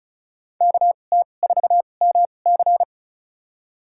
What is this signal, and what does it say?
Morse code, heard as a keyed tone, about 36 wpm, spaced as for 25 wpm, 690 Hz. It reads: KT4MC
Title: USA Call Signs - 5 Characters - 2x Character Spacing - 35wpm